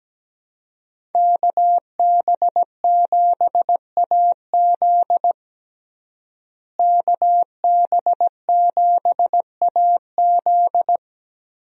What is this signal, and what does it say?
Morse code, heard as a keyed tone, about 17 wpm, 700 Hz. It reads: KB7AZ KB7AZ
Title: USA Call Signs - 5 Characters - Repeated 3x - 17wpm